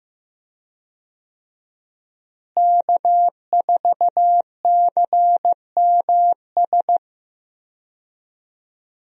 Morse code, heard as a keyed tone, about 15 wpm, 700 Hz. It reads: K4CMS
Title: USA Call Signs - 5 Characters - Repeated 2x - 15wpm